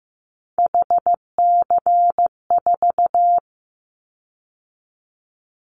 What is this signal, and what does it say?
Morse code, heard as a keyed tone, about 15 wpm, 700 Hz. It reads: HC4
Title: Callsign Prefixes for Top 100 DXCC - 15wpm